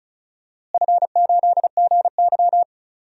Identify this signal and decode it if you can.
Morse code, heard as a keyed tone, about 34 wpm, 700 Hz. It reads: F8GY
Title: DXCC-100 Call Signs - 4 Characters - Repeated 2x - 35wpm